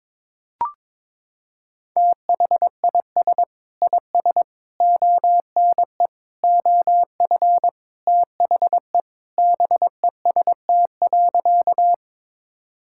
Morse code, heard as a keyed tone, about 22 wpm, 700 Hz. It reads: THIS IS ONE OF THE BEST.
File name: practice / 22wpm / Sentences with 4-Word Phrases - 22wpm